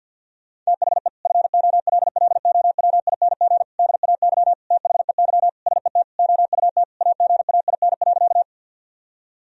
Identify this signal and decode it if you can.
Morse code, heard as a keyed tone, about 52 wpm, 700 Hz. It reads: THE FOLLOWING DAY THEY SET OUT AGAIN.
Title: Sentences from Top 1000 Words - Review Part 4 - 50wpm